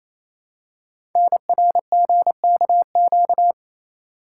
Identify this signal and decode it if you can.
Morse code, heard as a keyed tone, about 27 wpm, 700 Hz. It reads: NRGKQ
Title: Code Groups - 5 Letters - 28wpm